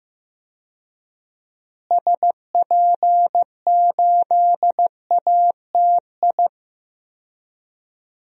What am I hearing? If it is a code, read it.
Morse code, heard as a keyed tone, about 15 wpm, 700 Hz. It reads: SP8ATI
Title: DXCC-100 Call Signs - 6 Characters - Repeated 3x - 15wpm